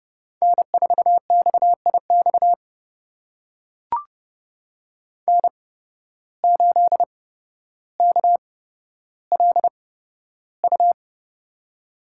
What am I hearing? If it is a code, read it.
Morse code, heard as a keyed tone, about 30 wpm, spaced as for 13 wpm, 700 Hz. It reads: N4XIX N8KLU
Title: USA Call Signs - 5 Characters - 8x Character Spacing - 30wpm